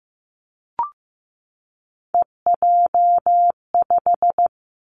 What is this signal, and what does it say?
Morse code, heard as a keyed tone, about 15 wpm, 700 Hz. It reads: EJ5
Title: Three Challenging Letter-Number Combinations 15wpm